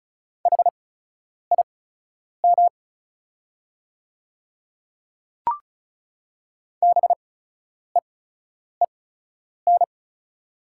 Morse code, heard as a keyed tone, about 34 wpm, spaced as for 10 wpm, 700 Hz. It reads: HIM BEEN
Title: 8x Character Spacing - Top 300 Words - 35wpm